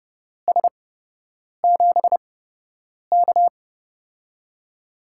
Morse code, heard as a keyed tone, about 30 wpm, spaced as for 8 wpm, 700 Hz. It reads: S7K
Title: USA Call Signs - 8x Character Spacing - 30wpm